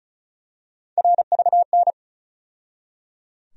Morse code, heard as a keyed tone, about 36 wpm, 700 Hz. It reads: RVN
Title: USA Call Sign Postfixes - 3 Letters 35wpm